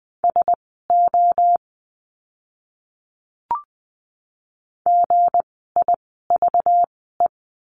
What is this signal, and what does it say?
Morse code, heard as a keyed tone, about 20 wpm, spaced as for 15 wpm, 700 Hz. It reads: SO GIVE
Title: 230 Lesson 37f - Letter x - Words 20wpm